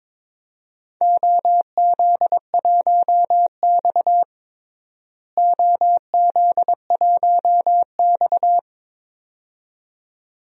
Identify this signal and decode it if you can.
Morse code, heard as a keyed tone, about 22 wpm, 700 Hz. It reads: OZ1X OZ1X
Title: DXCC-100 Call Signs - 4 Characters - Repeated 2x - 22wpm